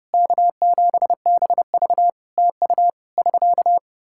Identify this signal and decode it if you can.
Morse code, heard as a keyed tone, about 30 wpm, 700 Hz. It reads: K7BV TU <SK>
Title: K1USN SST QSO Practice 30wpm